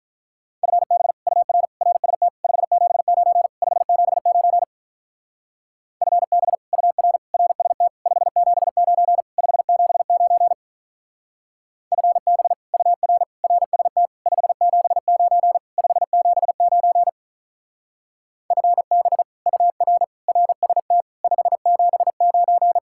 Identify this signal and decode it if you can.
Morse code, heard as a keyed tone, about 44 wpm, 700 Hz. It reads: FB UR RST 579 579 FB UR RST 579 579 FB UR RST 579 579 FB UR RST 579
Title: Speed-Racing Common QSO Phrases - 35wpm